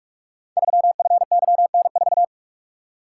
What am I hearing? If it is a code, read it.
Morse code, heard as a keyed tone, about 44 wpm, 700 Hz. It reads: 3FYN4
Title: Code Groups - 5 Letter:Numbers - 45wpm